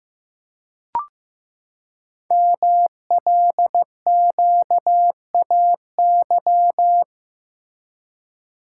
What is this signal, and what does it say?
Morse code, heard as a keyed tone, about 15 wpm, 700 Hz. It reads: MLQAY
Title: Code Groups - 5 Letters - 15wpm